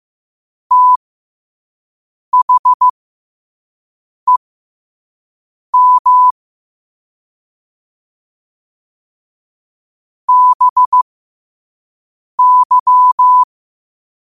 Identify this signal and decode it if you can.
Morse code, heard as a keyed tone, about 15 wpm, spaced as for 5 wpm, 1000 Hz. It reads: THEM BY
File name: qotdfiles / 5wpm